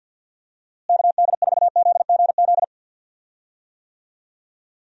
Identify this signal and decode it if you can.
Morse code, heard as a keyed tone, about 50 wpm, 700 Hz. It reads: KD4ZGZ
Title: USA Call Signs - 6 Characters - Repeated 3x - 50wpm